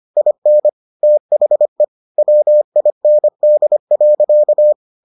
Morse code, heard as a keyed tone, about 25 wpm, 600 Hz. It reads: IN THE WIND.